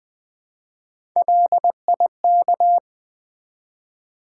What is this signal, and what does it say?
Morse code, heard as a keyed tone, about 20 wpm, 700 Hz. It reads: LIK
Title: Build A Word - 1 Letter At A Time - 3 to 5-Letter Words - 20wpm